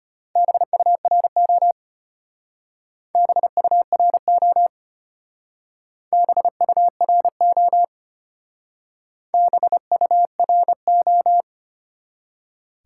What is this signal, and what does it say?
Morse code, heard as a keyed tone, about 34 wpm, 700 Hz. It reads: BURO BURO BURO BURO